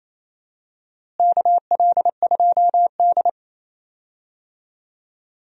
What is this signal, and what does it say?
Morse code, heard as a keyed tone, about 27 wpm, 700 Hz. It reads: KL2D
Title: USA Call Signs - 4 Characters - Repeated 2x - 28wpm